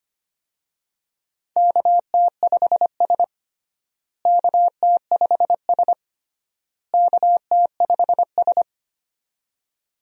Morse code, heard as a keyed tone, about 25 wpm, 700 Hz. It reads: KT5S KT5S KT5S
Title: USA Call Signs - 4 Characters - Repeated 3x - 25wpm